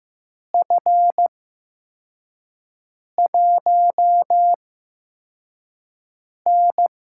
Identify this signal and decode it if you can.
Morse code, heard as a keyed tone, about 15 wpm, spaced as for 4 wpm, 700 Hz. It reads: F1N
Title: USA Call Signs - 4 Characters - 8x Character Spacing - 15wpm